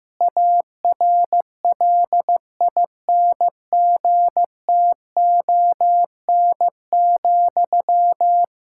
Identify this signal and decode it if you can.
Morse code, heard as a keyed tone, about 15 wpm, 700 Hz. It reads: ARLINGTON,
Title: Practice QSO - Exchange 1 - Beginning - 15wpm